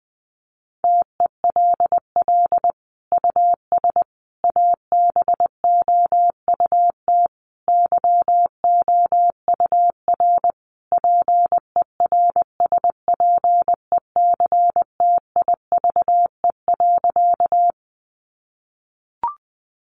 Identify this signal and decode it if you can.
Morse code, heard as a keyed tone, about 20 wpm, 700 Hz. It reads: TELL US ABOUT YOUR PERSPECTIVE.
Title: Sentences from Top 1200 Words 20wpm